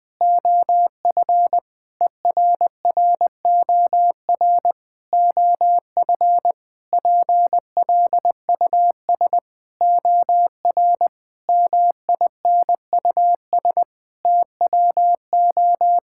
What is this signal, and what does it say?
Morse code, heard as a keyed tone, about 21 wpm, 700 Hz. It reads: OF ERROR OF PLUS OR MINUS TWO
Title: Sentences with 8-Word Phrases - 20wpm